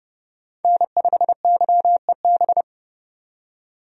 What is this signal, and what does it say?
Morse code, heard as a keyed tone, about 29 wpm, 700 Hz. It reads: N5YEB
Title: USA Call Signs - 5 Characters - 30wpm